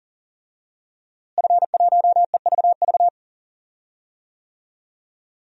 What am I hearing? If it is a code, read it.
Morse code, heard as a keyed tone, about 40 wpm, 700 Hz. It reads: F1EVV
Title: DXCC-100 Call Signs - 5 Characters - Repeated 2x - 40wpm